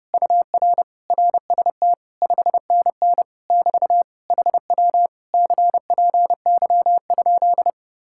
Morse code, heard as a keyed tone, about 30 wpm, 700 Hz. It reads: UR RST 5NN <BT> HW CPY?